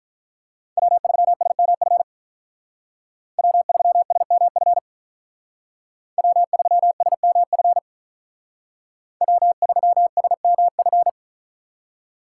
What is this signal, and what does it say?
Morse code, heard as a keyed tone, about 45 wpm, 700 Hz. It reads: W3SMF W3SMF W3SMF W3SMF